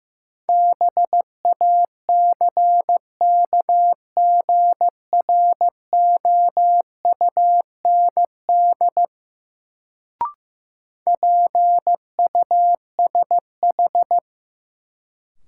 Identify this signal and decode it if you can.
Morse code, heard as a keyed tone, about 15 wpm, 700 Hz. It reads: BACKGROUND PUSH
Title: Sentences from Top 1300 Words - New Word Drill - 15wpm